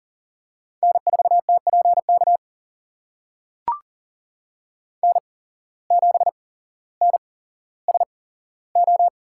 Morse code, heard as a keyed tone, about 40 wpm, spaced as for 17 wpm, 700 Hz. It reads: N4TPK N7NSO
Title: USA Call Signs - 5 Characters - 8x Character Spacing - 40wpm